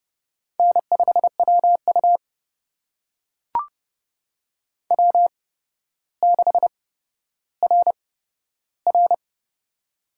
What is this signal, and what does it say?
Morse code, heard as a keyed tone, about 32 wpm, spaced as for 13 wpm, 700 Hz. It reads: N5WU W6RR
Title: USA Call Signs - 4 Characters - 8x Character Spacing - 30wpm